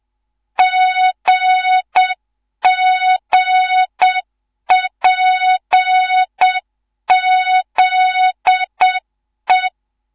Morse code, heard as a keyed tone, about 7 wpm, 3700 Hz, with noise. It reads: GGPZE
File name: Morze-HA2MN / hang / 19 GYAKORLAT